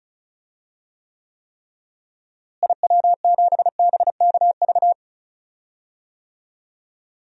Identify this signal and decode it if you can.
Morse code, heard as a keyed tone, about 34 wpm, 700 Hz. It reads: IW7BKV
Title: DXCC-100 Call Signs - 6 Characters - Repeated 3x - 35wpm